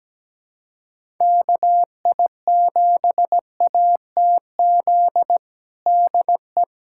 Morse code, heard as a keyed tone, about 17 wpm, 700 Hz. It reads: KI7ATZ DE
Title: USA Callsign DE Callsign - 6 Characters - 17wpm